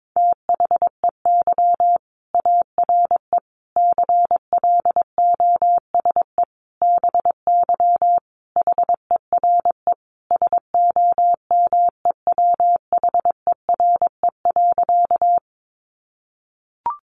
Morse code, heard as a keyed tone, about 22 wpm, 700 Hz. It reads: THEY ARE CLOSE BY HERE SOMEWHERE.
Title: Sentences from Top 1400 Words 22wpm